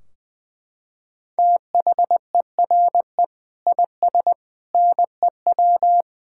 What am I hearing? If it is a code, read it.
Morse code, heard as a keyed tone, about 20 wpm, 700 Hz. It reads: THERE IS NEW
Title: Sentences from Top 1600 Words 20wpm